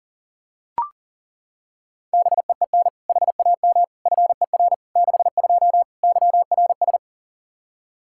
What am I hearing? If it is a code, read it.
Morse code, heard as a keyed tone, about 40 wpm, 700 Hz. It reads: BEEN HAM FER 62 YRS
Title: K1USN SST QSO Practice 40wpm